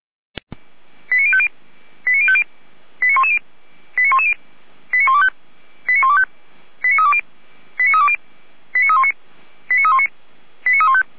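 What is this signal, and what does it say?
Five-tone selective call (ZVEI), heard as quick runs of five tones, tones of about 70 ms. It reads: 89050 89050 89110 89110 89125 89125 89320 89320 89319 89319 89315
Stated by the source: Received on an SDR.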